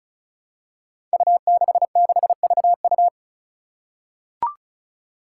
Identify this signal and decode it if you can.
Morse code, heard as a keyed tone, about 34 wpm, 700 Hz. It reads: U66VU
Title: Five Challenging Letter-Number Combinations 35wpm